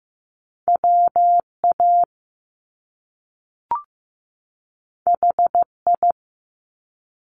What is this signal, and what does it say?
Morse code, heard as a keyed tone, about 15 wpm, 700 Hz. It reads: WA HI
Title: US States 15wpm